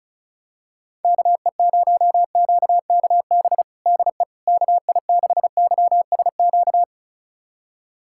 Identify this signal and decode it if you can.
Morse code, heard as a keyed tone, about 34 wpm, 700 Hz. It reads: KE0QKB DE KI6YSQ
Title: USA Callsign DE Callsign - 6 Characters - 35wpm